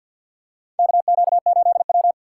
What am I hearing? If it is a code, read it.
Morse code, heard as a keyed tone, about 52 wpm, 700 Hz. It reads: KQ8W
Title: USA Call Signs - 4 Characters - 3x Character Spacing - 50wpm